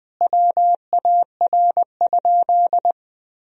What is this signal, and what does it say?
Morse code, heard as a keyed tone, about 20 wpm, 700 Hz. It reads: WAR?
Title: Sentences from Top 1000 Words - Review Part 2 - 20wpm